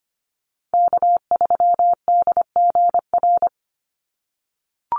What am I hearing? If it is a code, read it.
Morse code, heard as a keyed tone, about 25 wpm, 700 Hz. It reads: K3DGR
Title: Call Signs - 25wpm